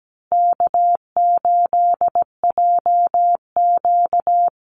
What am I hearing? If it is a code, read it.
Morse code, heard as a keyed tone, about 17 wpm, 700 Hz. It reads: K8JQ
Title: Easy Call Signs Repeated 3x - 17wpm